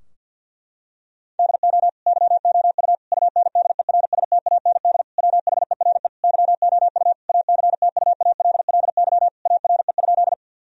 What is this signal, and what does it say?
Morse code, heard as a keyed tone, about 50 wpm, 700 Hz. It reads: DO YOU UNDERSTAND WHERE YOU ACTUALLY ARE?
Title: Sentences from Top 1000 Words - Review Part 4 - 50wpm